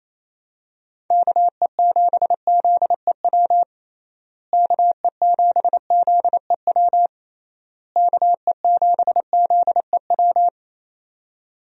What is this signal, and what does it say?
Morse code, heard as a keyed tone, about 28 wpm, 700 Hz. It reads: KE7ZEW KE7ZEW KE7ZEW
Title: USA Call Signs - 6 Characters - Repeated 3x - 28wpm